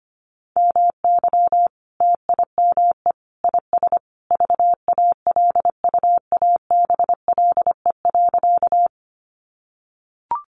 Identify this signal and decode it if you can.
Morse code, heard as a keyed tone, about 25 wpm, 700 Hz. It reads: MY TIME IS VALUABLE.
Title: Sentences from Top 1100 Words 25wpm